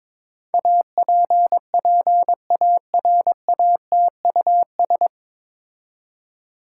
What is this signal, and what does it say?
Morse code, heard as a keyed tone, about 22 wpm, 700 Hz. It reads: APPARATUS
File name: 9 Letter Words - First 3 Letters Given - 22wpm